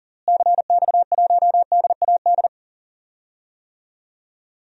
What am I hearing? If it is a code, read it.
Morse code, heard as a keyed tone, about 37 wpm, 700 Hz. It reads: CX1DAD
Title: DXCC-100 Call Signs - 6 Characters - Repeated 2x - 40wpm